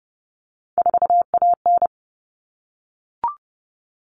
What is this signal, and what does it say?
Morse code, heard as a keyed tone, about 30 wpm, 700 Hz. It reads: 4AN
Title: All Cut Numbers - 3-Digit 30wpm